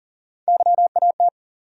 Morse code, heard as a keyed tone, about 40 wpm, 700 Hz. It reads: YAT
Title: USA Call Sign Postfixes - 3 Letters 40wpm